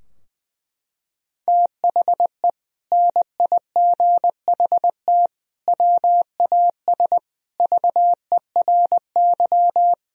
Morse code, heard as a keyed tone, about 20 wpm, 700 Hz. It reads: THE NIGHT WAS VERY
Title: Sentences from Top 1000 Words - Review Part 2 - 20wpm